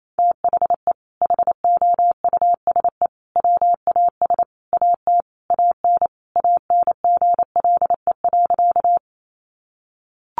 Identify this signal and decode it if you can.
Morse code, heard as a keyed tone, about 28 wpm, 700 Hz. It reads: THE HOUSE WAS AT AN ANGLE.
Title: Sentences from Top 1500 Words 28wpm